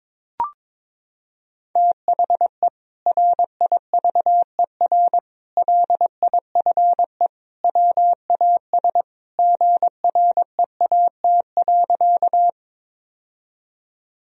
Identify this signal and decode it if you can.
Morse code, heard as a keyed tone, about 21 wpm, 700 Hz. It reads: THE RIVER LIFE WAS GREAT.
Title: Sentences from Top 1000 Words - Review Part 1 - 22wpm